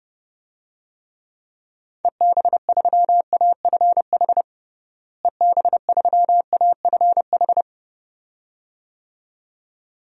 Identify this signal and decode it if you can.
Morse code, heard as a keyed tone, about 30 wpm, 700 Hz. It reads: EB3AFH EB3AFH
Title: DXCC-100 Call Signs - 6 Characters - Repeated 2x - 30wpm